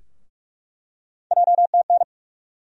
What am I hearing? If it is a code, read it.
Morse code, heard as a keyed tone, about 44 wpm, 700 Hz. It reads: JTN
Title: USA Call Sign Postfixes - 3 Letters 45wpm